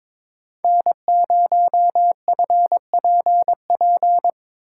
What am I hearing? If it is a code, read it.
Morse code, heard as a keyed tone, about 22 wpm, 700 Hz. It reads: N0FPP
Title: USA Call Signs - 5 Characters - 4x Character Spacing - 22wpm